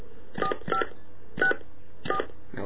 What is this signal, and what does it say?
DTMF, heard as telephone touch tones, tones of about 100 ms. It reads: 4564